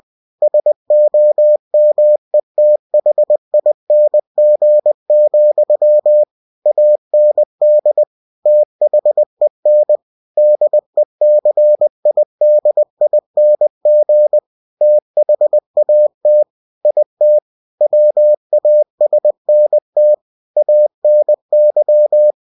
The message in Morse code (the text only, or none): SOMETHING, AND THEN DECIDING THAT IT WASNT ANY